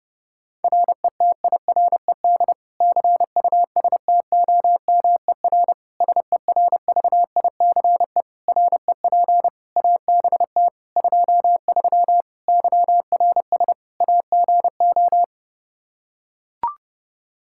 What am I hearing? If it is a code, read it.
Morse code, heard as a keyed tone, about 30 wpm, 700 Hz. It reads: RETIRED CUSTOMER SERVICE REP ABT 23 YRS AGO
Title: K1USN SST QSO Practice 30wpm